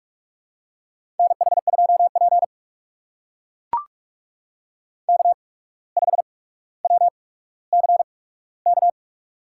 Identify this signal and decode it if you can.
Morse code, heard as a keyed tone, about 44 wpm, spaced as for 18 wpm, 700 Hz. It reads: NH2P K5WCK
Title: USA Call Signs - 8x Character Spacing - 45wpm